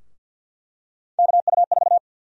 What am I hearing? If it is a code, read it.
Morse code, heard as a keyed tone, about 50 wpm, 700 Hz. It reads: KU4
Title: USA Call Sign Prefixes - 50wpm